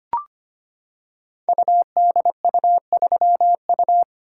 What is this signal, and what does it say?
Morse code, heard as a keyed tone, about 25 wpm, 700 Hz. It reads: UDU3U
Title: Code Groups - 5 Letter:Numbers - 25wpm